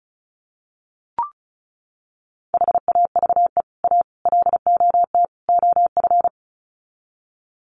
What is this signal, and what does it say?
Morse code, heard as a keyed tone, about 34 wpm, 700 Hz. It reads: HAVE A LOT OF
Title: Sets of 4 Words 35wpm